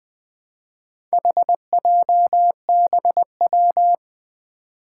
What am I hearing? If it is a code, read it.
Morse code, heard as a keyed tone, about 20 wpm, 700 Hz. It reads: HJBW
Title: Four Challenging Letter-Number Combinations 20wpm